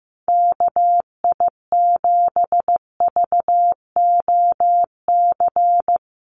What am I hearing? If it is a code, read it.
Morse code, heard as a keyed tone, about 15 wpm, 700 Hz. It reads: KI7VOC
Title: USA Call Signs Repeated 2x 15wpm